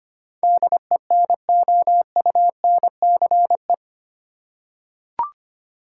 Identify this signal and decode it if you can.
Morse code, heard as a keyed tone, about 25 wpm, 700 Hz. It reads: DENOUNCE
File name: 8 Letter Words - First 3 Letters Given - 25wpm